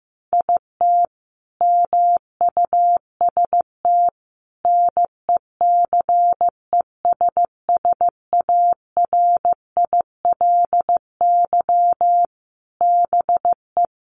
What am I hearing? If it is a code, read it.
Morse code, heard as a keyed tone, about 15 wpm, 700 Hz. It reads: IT MUST NECESSARILY BE
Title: Sentences from Top 1400 Words 15wpm